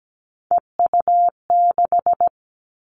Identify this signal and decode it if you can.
Morse code, heard as a keyed tone, about 17 wpm, 700 Hz. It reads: EU6
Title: Callsign Prefixes for Top 50 DXCC - 17wpm